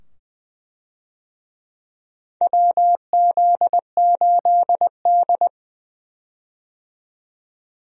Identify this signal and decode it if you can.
Morse code, heard as a keyed tone, about 20 wpm, 700 Hz. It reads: WZ8D
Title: USA Call Signs - 4 Characters - 20wpm